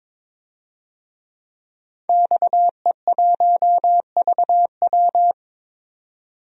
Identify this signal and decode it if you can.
Morse code, heard as a keyed tone, about 22 wpm, 700 Hz. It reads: XE1VW
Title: DXCC-100 Call Signs - 5 Characters - 22wpm